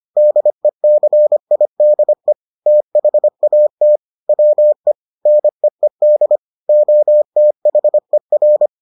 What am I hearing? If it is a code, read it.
Morse code, heard as a keyed tone, about 25 wpm, 600 Hz. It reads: DECIDE THAT WE NEED OTHER